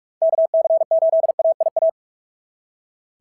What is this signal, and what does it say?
Morse code, heard as a keyed tone, about 46 wpm, 690 Hz. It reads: KC8AIA
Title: SOTA QSO Practice 45wpm